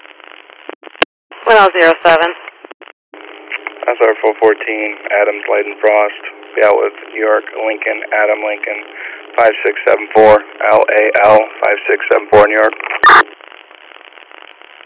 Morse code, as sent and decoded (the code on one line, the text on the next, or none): none